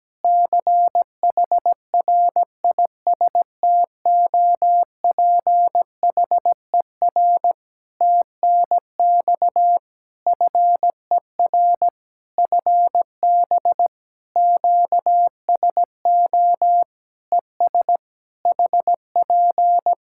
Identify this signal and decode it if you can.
Morse code, heard as a keyed tone, about 17 wpm, 700 Hz. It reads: CHRISTOPHER TNX FER FB QSO ES HP
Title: Practice QSO - Exchange 4 - Ending - 17wpm